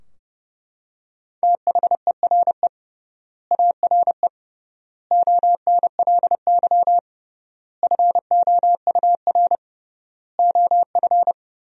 Morse code, heard as a keyed tone, about 30 wpm, 700 Hz. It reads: THERE ARE ONLY FOUR OF
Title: 3x Word-Spacing - Sentences from Top 300 Words - 30wpm